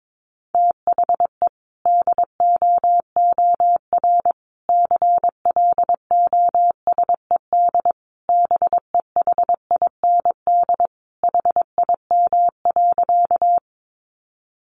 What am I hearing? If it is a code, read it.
Morse code, heard as a keyed tone, about 22 wpm, 700 Hz. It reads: THE DOOR CLOSED BEHIND HIM.
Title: Sentences from Top 1400 Words 22wpm